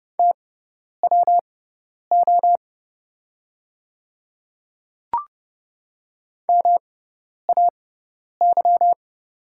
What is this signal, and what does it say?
Morse code, heard as a keyed tone, about 30 wpm, spaced as for 10 wpm, 690 Hz. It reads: TWO MAY
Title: 6x Character Spacing - Top 300 Words - 30wpm